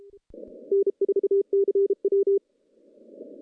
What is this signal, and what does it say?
Morse code, heard as a keyed tone, about 33 wpm, 400 Hz. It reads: N4CW
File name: contests / CW-SS-k5tr- -0100